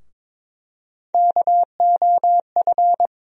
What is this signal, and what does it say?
Morse code, heard as a keyed tone, about 22 wpm, 700 Hz. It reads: KOF
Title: USA Call Sign Postfixes - 3 Letters 22wpm